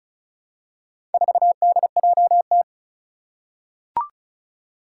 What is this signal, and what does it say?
Morse code, heard as a keyed tone, about 36 wpm, 700 Hz. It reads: 4DJT
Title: Four Challenging Letter-Number Combinations 35wpm